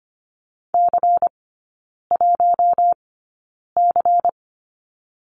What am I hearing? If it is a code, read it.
Morse code, heard as a keyed tone, about 25 wpm, spaced as for 9 wpm, 700 Hz. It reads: C1C
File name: Lesson 17e - Letter u - Three Characters 25wpm